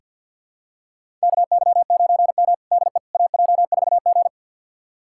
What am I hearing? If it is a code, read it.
Morse code, heard as a keyed tone, about 48 wpm, 700 Hz. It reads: KY9M DE AJ4G
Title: USA Callsign DE Callsign - 4 Characters - 50wpm